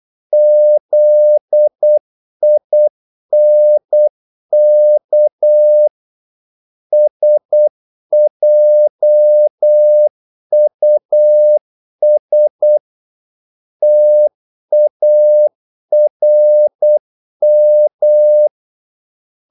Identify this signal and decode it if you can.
Morse code, heard as a keyed tone, about 8 wpm, 600 Hz. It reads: ZINK SJUS TARM
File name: Korte ord DK 8wpm